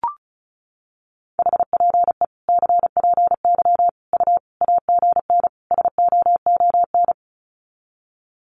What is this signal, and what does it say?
Morse code, heard as a keyed tone, about 34 wpm, 700 Hz. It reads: HPE CPY U AGN SOON
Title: Common QSO Phrases 35wpm